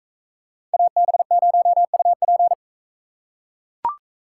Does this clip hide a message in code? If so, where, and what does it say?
Yes: Morse code from 0.7 to 2.5 s, heard as a keyed tone, about 43 wpm, 700 Hz. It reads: AB0UP